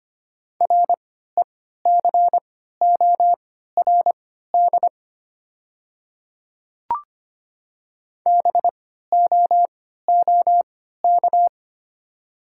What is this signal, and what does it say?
Morse code, heard as a keyed tone, about 25 wpm, spaced as for 14 wpm, 700 Hz. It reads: RECORD BOOK
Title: 3x Character Spacing - Top 500 Words - 25wpm